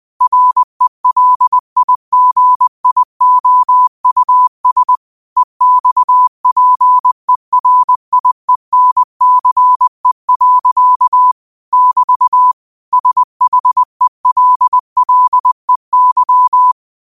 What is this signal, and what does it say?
Morse code, heard as a keyed tone, about 20 wpm, 1000 Hz. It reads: RELIGIOUS EXPERIENCE. <BT> SHELLEY